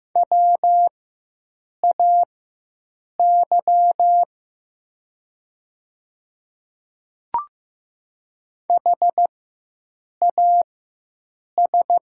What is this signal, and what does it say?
Morse code, heard as a keyed tone, about 15 wpm, spaced as for 7 wpm, 700 Hz. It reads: WAY HAS
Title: 4x Character Spacing - Top 100 Words - 15wpm